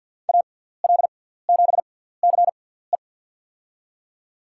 Morse code, heard as a keyed tone, about 52 wpm, spaced as for 17 wpm, 700 Hz. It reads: AL7CE
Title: USA Call Signs - 6x Character Spacing - 50wpm